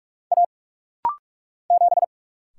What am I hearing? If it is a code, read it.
Morse code, heard as a keyed tone, about 46 wpm, 700 Hz. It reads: A 7
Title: Single Letter-Number - Rapid-Fire 45wpm